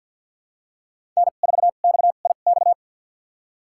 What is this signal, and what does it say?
Morse code, heard as a keyed tone, about 53 wpm, spaced as for 37 wpm, 700 Hz. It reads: N4XIX
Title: USA Call Signs - 5 Characters - 2x Character Spacing - 50wpm